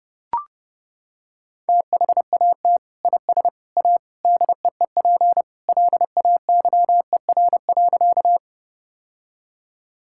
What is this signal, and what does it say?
Morse code, heard as a keyed tone, about 32 wpm, 700 Hz. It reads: THAT IS A DEEP LAYER.
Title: Sentences from Top 1600 Words 30wpm